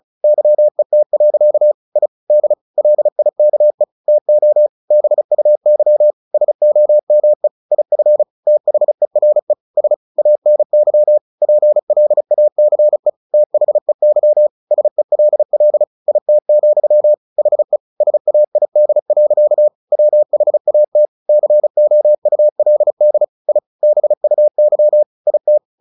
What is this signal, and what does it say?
Morse code, heard as a keyed tone, about 34 wpm, 600 Hz. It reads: YET. I D LIKE TO BUY SOME IF THERE S ANY PLACE THEY SELL IT, HE SAID. WHAT COULD I BUY IT
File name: the old man and the sea 2 35wpm